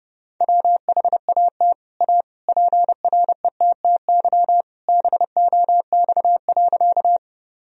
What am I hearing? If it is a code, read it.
Morse code, heard as a keyed tone, about 30 wpm, 700 Hz. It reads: WHAT A PRETTY BOX.
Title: Sentences from Top 1000 Words - Review Part 2 - 30wpm